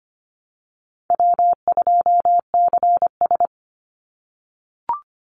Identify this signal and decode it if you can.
Morse code, heard as a keyed tone, about 25 wpm, 700 Hz. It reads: W2CS
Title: Easy Call Signs - 25wpm